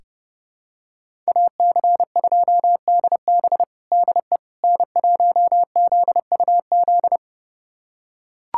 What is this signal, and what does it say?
Morse code, heard as a keyed tone, about 32 wpm, 700 Hz. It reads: AC2DB DE N1ZUZ
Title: USA Callsign DE Callsign - 5 Characters - 30wpm